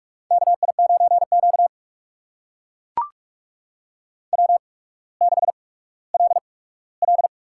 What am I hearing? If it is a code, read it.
Morse code, heard as a keyed tone, about 44 wpm, spaced as for 19 wpm, 700 Hz. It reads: KI9Q W6LL
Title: USA Call Signs - 4 Characters - 8x Character Spacing - 45wpm